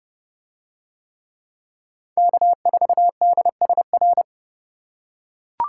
Morse code, heard as a keyed tone, about 29 wpm, 700 Hz. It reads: K4DSR